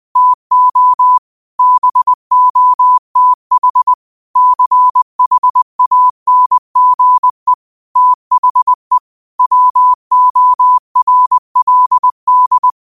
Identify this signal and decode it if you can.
Morse code, heard as a keyed tone, about 20 wpm, 1000 Hz. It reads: TO BOTH CHANGE THE WORLD
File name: qotdfiles / 20wpm